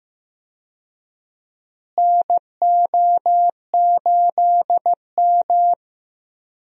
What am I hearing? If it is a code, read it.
Morse code, heard as a keyed tone, about 15 wpm, 700 Hz. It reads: NO8M
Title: USA Call Signs - 4 Characters - Repeated 2x - 15wpm